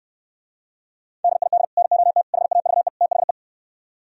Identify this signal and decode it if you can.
Morse code, heard as a keyed tone, about 67 wpm, 700 Hz. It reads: DID NOT HAVE THE